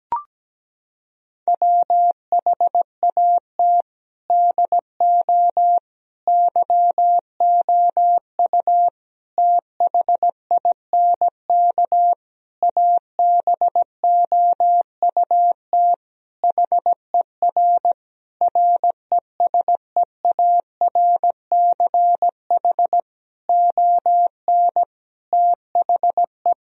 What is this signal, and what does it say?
Morse code, heard as a keyed tone, about 17 wpm, 700 Hz. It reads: WHAT DO YOU THINK ABOUT HER RESEARCH ON THE
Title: Sentences from Top 1000 Words - Review Part 4 - 17wpm